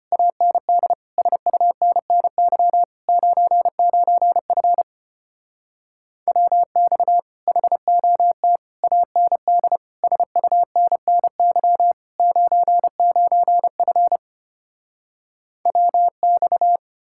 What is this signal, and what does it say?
Morse code, heard as a keyed tone, about 33 wpm, 700 Hz. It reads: AND SUNNY 99F WX HOT AND SUNNY 99F WX